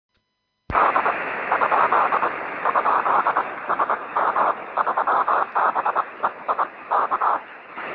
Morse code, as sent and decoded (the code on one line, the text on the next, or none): none